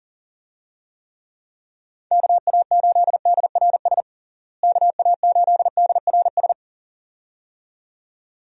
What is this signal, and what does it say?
Morse code, heard as a keyed tone, about 40 wpm, 700 Hz. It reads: KA8DRS KA8DRS
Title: USA Call Signs - 6 Characters - Repeated 2x - 40wpm